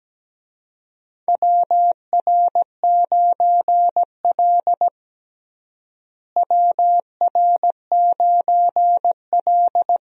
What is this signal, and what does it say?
Morse code, heard as a keyed tone, about 17 wpm, 700 Hz. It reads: WR9L WR9L
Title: USA Call Signs - 4 Characters - Repeated 3x - 17wpm